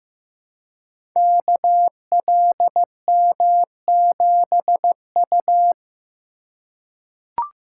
Morse code, heard as a keyed tone, about 15 wpm, 700 Hz. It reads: KLM7U
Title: Code Groups - 5 Letter:Numbers - 15wpm